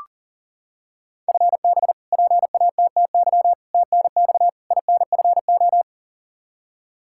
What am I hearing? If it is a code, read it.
Morse code, heard as a keyed tone, about 40 wpm, 700 Hz. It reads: FB PATTY TNX INFO
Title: K1USN SST QSO Practice 40wpm